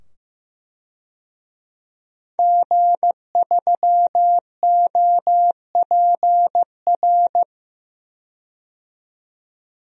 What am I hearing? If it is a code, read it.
Morse code, heard as a keyed tone, about 15 wpm, 700 Hz. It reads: G3OPR
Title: DXCC-100 Call Signs - 5 Characters - 15wpm